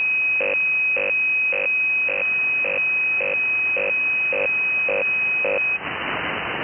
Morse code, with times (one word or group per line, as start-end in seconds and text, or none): none